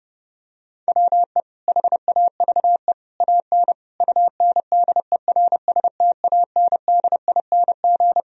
Morse code, heard as a keyed tone, about 32 wpm, 700 Hz. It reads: WE HAVE AN UNDERSTANDING
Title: Sentences from Top 1000 Words - Review Part 3 - 30wpm